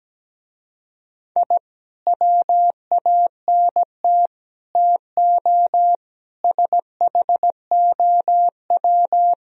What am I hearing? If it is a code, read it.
Morse code, heard as a keyed tone, about 17 wpm, 700 Hz. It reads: I WANT TO SHOW
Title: Speed-Racing - Sets of 5 Words 17wpm